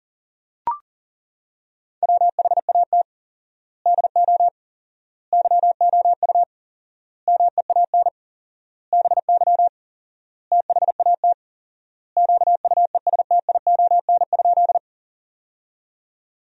Morse code, heard as a keyed tone, about 37 wpm, 700 Hz. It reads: WHAT DO YOU MEAN BY THAT QUESTION?
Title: 4x Word-Spacing - Sentences from Top 400 Words - 40wpm